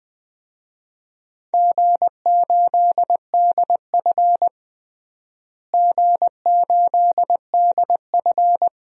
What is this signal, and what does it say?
Morse code, heard as a keyed tone, about 21 wpm, 700 Hz. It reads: G8DF G8DF
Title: DXCC-100 Call Signs - 4 Characters - Repeated 3x - 20wpm